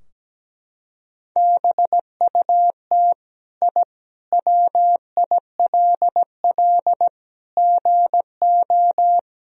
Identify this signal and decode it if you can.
Morse code, heard as a keyed tone, about 17 wpm, 700 Hz. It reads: BUT I WILL GO
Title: Sentences from Top 1000 Words - Review Part 3 - 17wpm